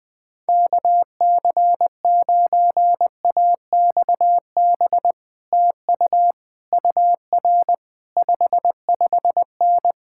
Morse code, heard as a keyed tone, about 20 wpm, 700 Hz. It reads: KC9AXB TU UR 55N
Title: Speed-Racing Common QSO Phrases - 17wpm